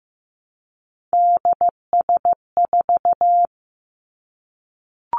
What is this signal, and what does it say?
Morse code, heard as a keyed tone, about 15 wpm, 700 Hz. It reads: DS4
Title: Three Challenging Letter-Number Combinations 15wpm